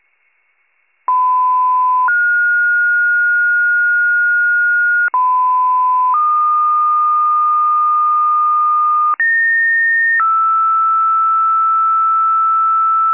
Two-tone page, heard as a one-second tone, then a three-second tone, about 1010 then 1430 Hz, about 1010 then 1220 Hz, about 1850 then 1320 Hz. It is heard from 1.1 to 13.1 s.